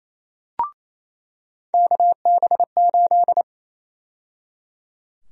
Morse code, heard as a keyed tone, about 28 wpm, 700 Hz. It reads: KB8
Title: USA Call Sign Prefixes - 28wpm